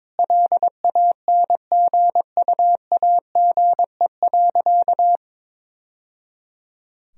Morse code, heard as a keyed tone, about 22 wpm, 700 Hz. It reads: LANGUAGE.
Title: Sentences from Top 1000 Words - Review Part 3 - 22wpm